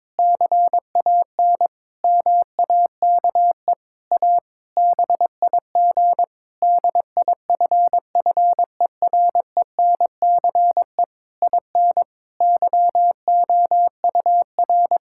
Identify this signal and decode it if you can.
Morse code, heard as a keyed tone, about 21 wpm, 700 Hz. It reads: CAN MAKE A BIG DIFFERENCE IN YOUR
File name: Difficult Sentences with Hints - 22wpm